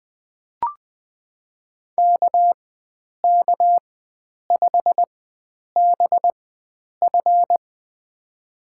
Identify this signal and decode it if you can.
Morse code, heard as a keyed tone, about 20 wpm, spaced as for 9 wpm, 700 Hz. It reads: KK5BF
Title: USA Call Signs - 5 Characters - 4x Character Spacing - 20wpm